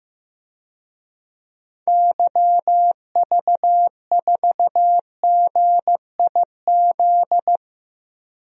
Morse code, heard as a keyed tone, about 15 wpm, 700 Hz. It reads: YV4GIZ
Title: DXCC-100 Call Signs - 6 Characters - Repeated 2x - 15wpm